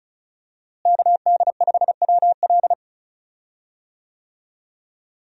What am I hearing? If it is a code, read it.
Morse code, heard as a keyed tone, about 34 wpm, 700 Hz. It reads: KD5WL
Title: USA Call Signs - 5 Characters - Repeated 2x - 35wpm